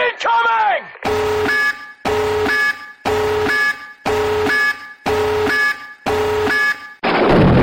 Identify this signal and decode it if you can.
Busy tone (Europe, 425 Hz), with speech.
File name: tactical-nukez